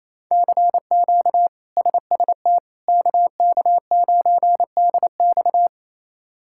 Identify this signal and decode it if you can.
Morse code, heard as a keyed tone, about 28 wpm, 700 Hz. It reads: CQ SST KK9DX
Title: K1USN SST QSO Practice - 28wpm